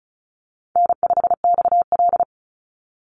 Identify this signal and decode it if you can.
Morse code, heard as a keyed tone, about 34 wpm, 700 Hz. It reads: N5XL
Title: NA SOTA Chaser Callsigns 35wpm